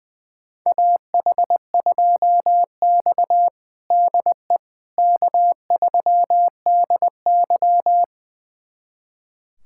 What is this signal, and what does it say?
Morse code, heard as a keyed tone, about 20 wpm, 700 Hz. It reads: AH2X DE K3DY
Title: USA Callsign DE Callsign - 4 Characters - 20wpm